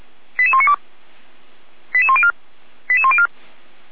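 Five-tone selective call (ZVEI), heard as quick runs of five tones, tones of about 80 ms. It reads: 88192 88104 88104
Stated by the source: Five-tone ZVEI selective call received by SDR, from the feuerwehr-orlamuende.org (F85360) audio server.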